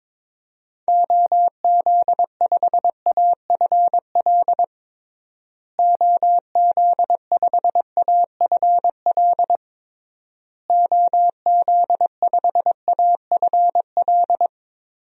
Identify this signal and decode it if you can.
Morse code, heard as a keyed tone, about 22 wpm, 700 Hz. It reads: OZ5AFL OZ5AFL OZ5AFL
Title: DXCC-100 Call Signs - 6 Characters - Repeated 3x - 22wpm